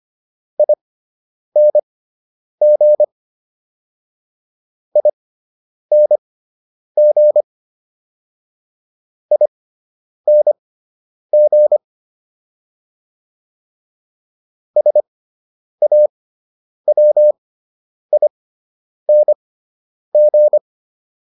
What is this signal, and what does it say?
Morse code, heard as a keyed tone, about 25 wpm, spaced as for 9 wpm, 600 Hz. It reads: ING ING ING SAWING
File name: ING6 09